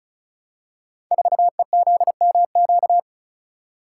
Morse code, heard as a keyed tone, about 36 wpm, 700 Hz. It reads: 4EZMQ
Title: Code Groups - 5 Letter:Numbers - 35wpm